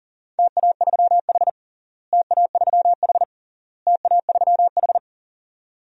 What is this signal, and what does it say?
Morse code, heard as a keyed tone, about 37 wpm, 700 Hz. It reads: TA3H TA3H TA3H
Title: DXCC-100 Call Signs - 4 Characters - Repeated 3x - 40wpm